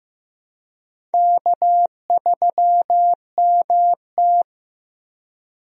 Morse code, heard as a keyed tone, about 15 wpm, 700 Hz. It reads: K3MT
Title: USA Call Signs - 4 Characters - Repeated 2x - 15wpm